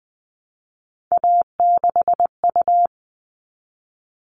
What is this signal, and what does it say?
Morse code, heard as a keyed tone, about 20 wpm, 700 Hz. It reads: A6U
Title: All Cut Numbers - 3-Digit 20wpm